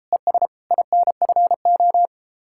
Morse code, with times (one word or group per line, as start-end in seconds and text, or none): 0.1-0.5 ES
0.7-2.1 INFO